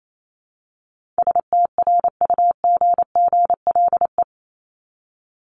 Morse code, heard as a keyed tone, about 28 wpm, 700 Hz. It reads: STRUGGLE
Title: Top 1751-2000 Words 28wpm